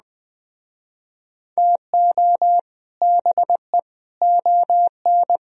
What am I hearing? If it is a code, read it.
Morse code, heard as a keyed tone, about 20 wpm, 700 Hz. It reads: TO BE ON
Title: Speed-Racing - Sets of 4 Words 17wpm